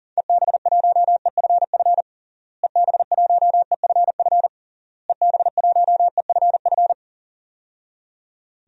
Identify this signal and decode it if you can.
Morse code, heard as a keyed tone, about 40 wpm, 700 Hz. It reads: EB1EFF EB1EFF EB1EFF
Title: DXCC-100 Call Signs - 6 Characters - Repeated 3x - 40wpm